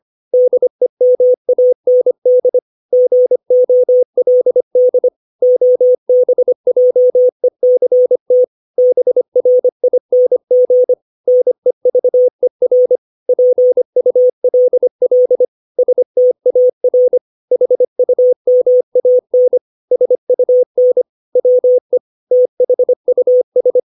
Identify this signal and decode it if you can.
Morse code, heard as a keyed tone, about 25 wpm, 500 Hz. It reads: DEMAND GOLD OBJECT BRING NEVER PULL STAR HUMAN SUN WE THUS
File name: 632 words / 632 Words 25WPM